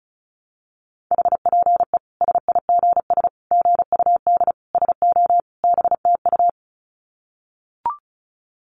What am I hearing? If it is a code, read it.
Morse code, heard as a keyed tone, about 34 wpm, 700 Hz. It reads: HPE SIGS GUD SO BTU
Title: Common QSO Phrases 35wpm